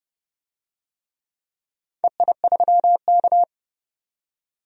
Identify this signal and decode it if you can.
Morse code, heard as a keyed tone, about 30 wpm, 700 Hz. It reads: EI3K
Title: DXCC-100 Call Signs - 4 Characters - 30wpm